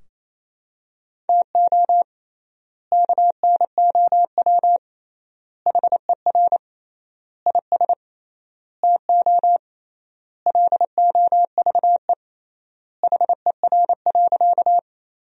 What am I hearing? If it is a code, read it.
Morse code, heard as a keyed tone, about 28 wpm, 700 Hz. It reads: TO KNOW HER IS TO LOVE HER.
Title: 3x Word-Spacing - Sentences from Top 400 Words - 28wpm